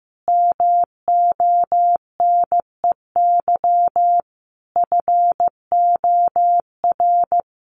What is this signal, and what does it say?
Morse code, heard as a keyed tone, about 14 wpm, 700 Hz. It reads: MONEY FOR
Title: Sentences from Top 1400 Words 15wpm